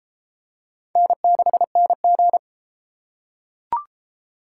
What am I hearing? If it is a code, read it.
Morse code, heard as a keyed tone, about 29 wpm, 700 Hz. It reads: N6NG